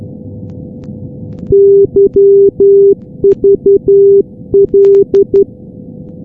Morse code, heard as a keyed tone, about 11 wpm, 400 Hz. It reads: YVL
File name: europe / g-443-yvl-343-sdr-hu-g-south west uk- -1245z